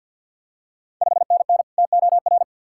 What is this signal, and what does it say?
Morse code, heard as a keyed tone, about 50 wpm, 690 Hz. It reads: 5NN TOR
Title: Ontario QSO Party Exchange 50wpm